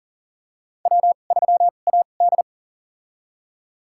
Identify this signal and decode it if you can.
Morse code, heard as a keyed tone, about 37 wpm, spaced as for 28 wpm, 700 Hz. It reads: W3AD
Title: USA Call Signs - 4 Characters - 2x Character Spacing - 40wpm